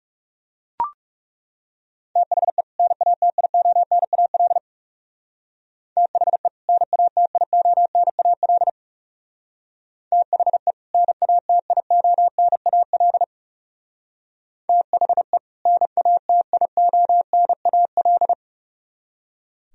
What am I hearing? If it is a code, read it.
Morse code, heard as a keyed tone, about 37 wpm, 700 Hz. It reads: THE NATIONAL THE NATIONAL THE NATIONAL THE NATIONAL
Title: Speed-Racing - Sets of 2 Words 30wpm